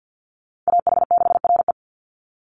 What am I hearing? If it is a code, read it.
Morse code, heard as a keyed tone, about 50 wpm, 700 Hz. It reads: AH6RE
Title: Call Signs - 50wpm